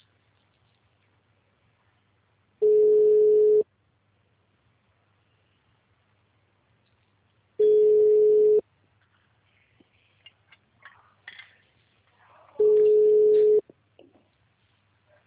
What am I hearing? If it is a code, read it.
Ringback tone (Europe, 425 Hz).